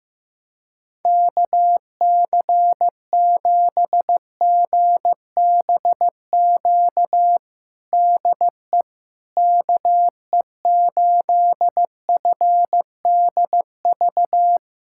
Morse code, heard as a keyed tone, about 15 wpm, 700 Hz. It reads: KC7GBQ DE KE8FDV
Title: USA Callsign DE Callsign - 6 Characters - 15wpm